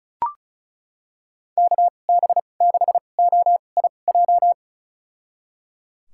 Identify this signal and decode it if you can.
Morse code, heard as a keyed tone, about 36 wpm, spaced as for 25 wpm, 700 Hz. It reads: KB6OIJ
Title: USA Call Signs - 6 Characters - 2x Character Spacing - 35wpm